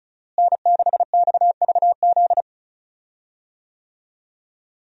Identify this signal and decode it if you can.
Morse code, heard as a keyed tone, about 34 wpm, 700 Hz. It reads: N6XVZ
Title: USA Call Signs - 5 Characters - Repeated 3x - 35wpm